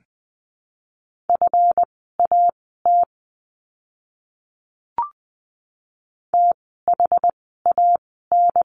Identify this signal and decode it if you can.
Morse code, heard as a keyed tone, about 20 wpm, spaced as for 14 wpm, 700 Hz. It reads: FAT THAN
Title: 139 Lesson 23f - Letter f - Words 20wpm